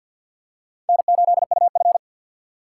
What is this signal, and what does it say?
Morse code, heard as a keyed tone, about 50 wpm, 700 Hz. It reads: N8UF
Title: USA Call Signs - 3x Character Spacing - 50wpm